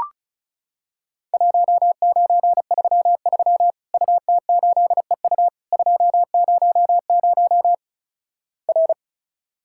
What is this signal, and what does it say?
Morse code, heard as a keyed tone, about 36 wpm, 700 Hz. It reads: 1933 UT8EU 200 R
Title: Worked All Europe DX Contest - QTC Exchange - 35wpm